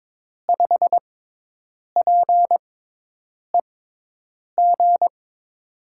Morse code, heard as a keyed tone, about 22 wpm, spaced as for 8 wpm, 700 Hz. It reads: 5PEG
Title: USA Call Signs - 5 Characters - 6x Character Spacing - 22wpm